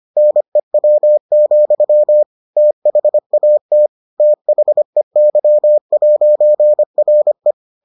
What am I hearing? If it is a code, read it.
Morse code, heard as a keyed tone, about 25 wpm, 600 Hz. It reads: NEW, THAT THEY'RE